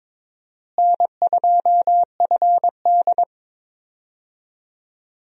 Morse code, heard as a keyed tone, about 22 wpm, 700 Hz. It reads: N2FD
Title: USA Call Signs - 4 Characters - Repeated 3x - 22wpm